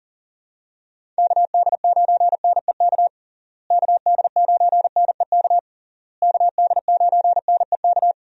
Morse code, heard as a keyed tone, about 37 wpm, 700 Hz. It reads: KD9NEK KD9NEK KD9NEK
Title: USA Call Signs - 6 Characters - Repeated 3x - 40wpm